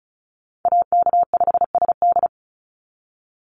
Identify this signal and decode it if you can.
Morse code, heard as a keyed tone, about 34 wpm, 700 Hz. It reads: AK5SD
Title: NA SOTA Chaser Callsigns 35wpm